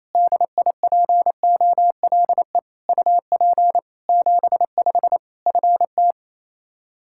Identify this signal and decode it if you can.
Morse code, heard as a keyed tone, about 27 wpm, 700 Hz. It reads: DIPOLE UP 75 FT
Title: Practice QSO - Exchange 2 - Info - 28wpm